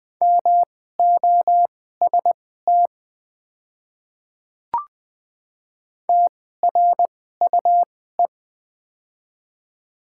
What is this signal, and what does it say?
Morse code, heard as a keyed tone, about 20 wpm, spaced as for 14 wpm, 700 Hz. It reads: MOST TRUE
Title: 2x Character Spacing - Top 500 Words - 20wpm